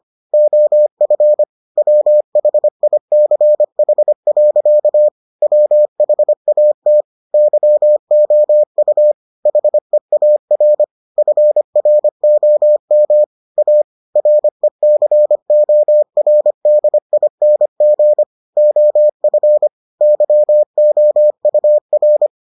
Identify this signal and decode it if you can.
Morse code, heard as a keyed tone, about 25 wpm, 600 Hz. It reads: OF WHICH. WHAT YOU HEAR FROM A RECORDING OF YOUR